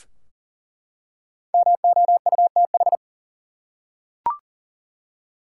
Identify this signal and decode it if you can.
Morse code, heard as a keyed tone, about 38 wpm, 700 Hz. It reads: MOUTH
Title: Five Letter Words - 40wpm